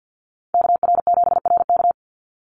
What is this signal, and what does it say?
Morse code, heard as a keyed tone, about 48 wpm, 700 Hz. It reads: KR7RK
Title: NA SOTA Chaser Callsigns 50wpm